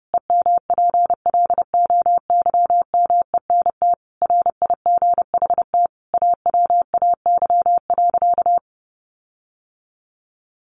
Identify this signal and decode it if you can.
Morse code, heard as a keyed tone, about 30 wpm, 700 Hz. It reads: EMPLOYMENT RIGHT AWAY.
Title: Sentences from Top 1500 Words 30wpm